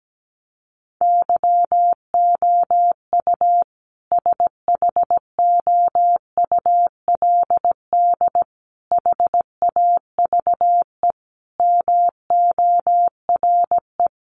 Morse code, heard as a keyed tone, about 17 wpm, 700 Hz. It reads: YOU SHOULD HAVE MORE
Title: Sentences from Top 1300 Words 17wpm